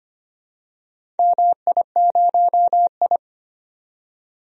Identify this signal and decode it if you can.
Morse code, heard as a keyed tone, about 25 wpm, 690 Hz. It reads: MI0I
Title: DXCC-100 Call Signs - 4 Characters - 25wpm